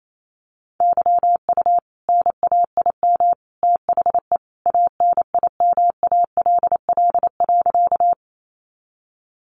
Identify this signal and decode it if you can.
Morse code, heard as a keyed tone, about 28 wpm, 700 Hz. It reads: YU NAIM THE ANIMALL.
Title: A Sound Of Thunder 28wpm